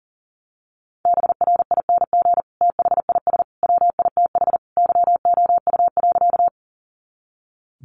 Morse code, heard as a keyed tone, about 40 wpm, 700 Hz. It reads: BRING THIS WITH YOU.
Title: Sentences from Top 500 Words 40wpm